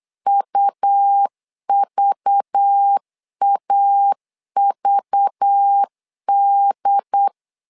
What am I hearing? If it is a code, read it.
Morse code, heard as a keyed tone, about 8 wpm, 800 Hz. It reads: UVAVD
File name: LEZIONI / 5 LBVC / 5 LBV 8wpm 2837